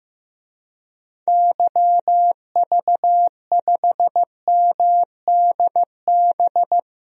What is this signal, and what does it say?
Morse code, heard as a keyed tone, about 15 wpm, 700 Hz. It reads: YV5MDB
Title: DXCC-100 Call Signs - 6 Characters - Repeated 2x - 15wpm